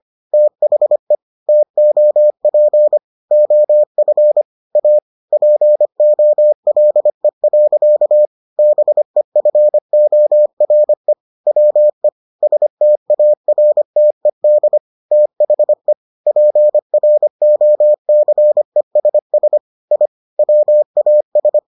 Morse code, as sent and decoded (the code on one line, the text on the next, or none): - .... / - --- .--. / --- ..-. / .- / .--. --- .-.. .-.-.- / -... ..-. --- .-. / .-- . / ... - .- .-. - . -.. / - .... / .--. .-. --- -.-. ... ... / .. / .-- .- ...
THE TOP OF A POLE. BEFORE WE STARTED THE PROCESS I WAS